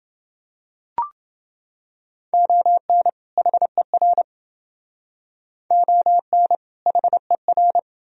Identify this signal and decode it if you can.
Morse code, heard as a keyed tone, about 29 wpm, 700 Hz. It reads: ON HER ON HER